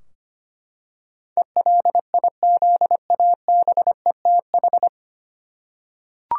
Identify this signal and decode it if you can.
Morse code, heard as a keyed tone, about 25 wpm, 700 Hz. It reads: ELIZABETH
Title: USA Common Names - 25wpm